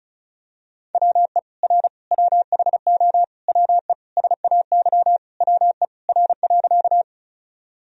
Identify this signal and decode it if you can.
Morse code, heard as a keyed tone, about 36 wpm, 700 Hz. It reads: WE R WHO WE SAY WE R.
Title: Sentences from Top 200 Words and Abbreviations 35wpm